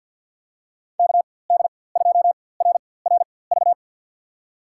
Morse code, heard as a keyed tone, about 51 wpm, spaced as for 23 wpm, 700 Hz. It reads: KD2RRV
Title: USA Call Signs - 6 Characters - 4x Character Spacing - 50wpm